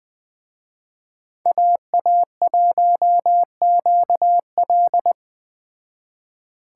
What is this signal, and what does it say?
Morse code, heard as a keyed tone, about 20 wpm, 700 Hz. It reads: AA1QL